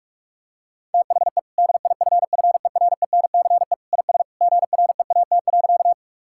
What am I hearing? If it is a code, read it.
Morse code, heard as a keyed tone, about 46 wpm, 700 Hz. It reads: THE DIFFERENCE IS GREAT.
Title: Sentences from Top 1000 Words - Review Part 5 - 45wpm